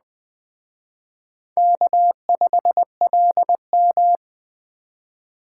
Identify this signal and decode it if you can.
Morse code, heard as a keyed tone, about 20 wpm, 700 Hz. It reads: K5LM
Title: Speed Racing - USA 1x2 Call Signs 17wpm